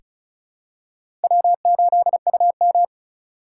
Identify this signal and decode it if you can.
Morse code, heard as a keyed tone, about 36 wpm, 690 Hz. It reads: W8UM
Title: USA Call Signs - 3x Character Spacing - 35wpm